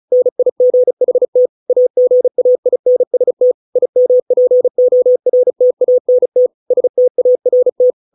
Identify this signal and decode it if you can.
Morse code, heard as a keyed tone, about 36 wpm, 500 Hz. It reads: NIGHT AGAINST IMPORTANT START
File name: 632 words / 632 Words 35WPM